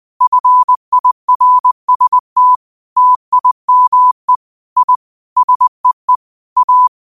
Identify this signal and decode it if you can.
Morse code, heard as a keyed tone, about 20 wpm, 1000 Hz. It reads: FIRST TIME I SEE A
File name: qotdfiles / 20wpm